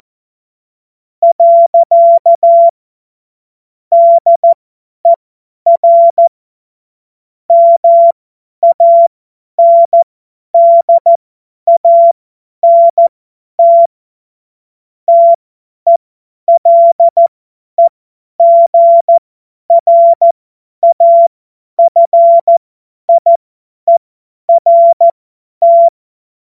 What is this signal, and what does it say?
Morse code, heard as a keyed tone, about 14 wpm, spaced as for 10 wpm, 680 Hz. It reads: . DER MANDANT TELEGRAFIERT
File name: Morsekurs / Witze / Witz 09. August 2022 50 F70